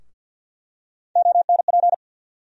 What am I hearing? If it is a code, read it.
Morse code, heard as a keyed tone, about 47 wpm, 690 Hz. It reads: ONP
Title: USA Call Sign Postfixes - 3 Letters 50wpm